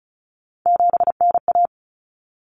Morse code, heard as a keyed tone, about 34 wpm, 700 Hz. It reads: 7NA
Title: Common Cut Numbers - 3-Digit 35wpm